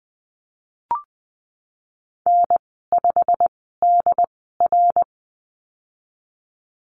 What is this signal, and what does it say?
Morse code, heard as a keyed tone, about 20 wpm, spaced as for 15 wpm, 700 Hz. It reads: N5DR
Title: 088 Lesson 15g - Number 5 - Call Signs 20wpm